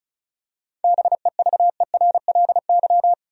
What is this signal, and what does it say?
Morse code, heard as a keyed tone, about 36 wpm, 700 Hz. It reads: BEVERLY
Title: USA Common Names - 35wpm